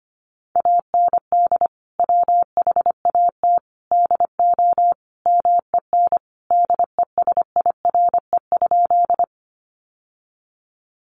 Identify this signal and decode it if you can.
Morse code, heard as a keyed tone, about 25 wpm, 700 Hz. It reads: AND WHAT DO MEN DESIRE?
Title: Sentences from Top 1100 Words 25wpm